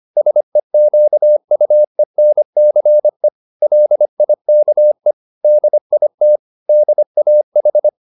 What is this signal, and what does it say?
Morse code, heard as a keyed tone, about 25 wpm, 600 Hz. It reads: SEQUENCE LIKE DIT DAH